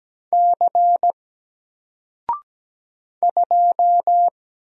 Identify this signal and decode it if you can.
Morse code, heard as a keyed tone, about 17 wpm, 700 Hz. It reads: C 2
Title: Single Letter-Number Alt 17wpm